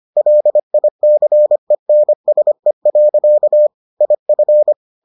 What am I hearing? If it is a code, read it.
Morse code, heard as a keyed tone, about 25 wpm, 600 Hz. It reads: LICENSE. IF